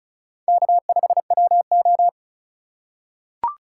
Morse code, heard as a keyed tone, about 36 wpm, 700 Hz. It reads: K5WO
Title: USA Call Signs - 4 Characters - 4x Character Spacing - 35wpm